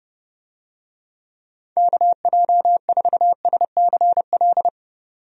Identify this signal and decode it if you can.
Morse code, heard as a keyed tone, about 30 wpm, 700 Hz. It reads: KJ4SCL
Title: USA Call Signs - 6 Characters - 30wpm